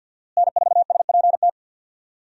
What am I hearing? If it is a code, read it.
Morse code, heard as a keyed tone, about 52 wpm, 700 Hz. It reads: N4SPT
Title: USA Call Signs - 5 Characters - Repeated 2x - 50wpm